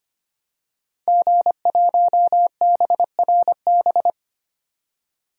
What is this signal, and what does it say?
Morse code, heard as a keyed tone, about 25 wpm, 700 Hz. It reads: G1BRB
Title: DXCC-100 Call Signs - 5 Characters - 25wpm